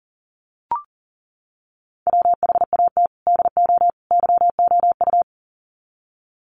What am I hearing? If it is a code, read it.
Morse code, heard as a keyed tone, about 40 wpm, 700 Hz. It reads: WHAT DO YOU
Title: Sets of 3 Words 40wpm